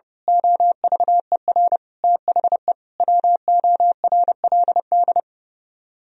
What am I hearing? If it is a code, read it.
Morse code, heard as a keyed tone, about 28 wpm, 700 Hz. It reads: OVER THE WORLD
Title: Speed-Racing - Sets of 5 Words 30wpm